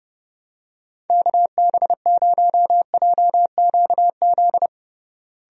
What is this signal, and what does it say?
Morse code, heard as a keyed tone, about 32 wpm, 700 Hz. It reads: KB0JQZ
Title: USA Call Signs - 6 Characters - Repeated 3x - 30wpm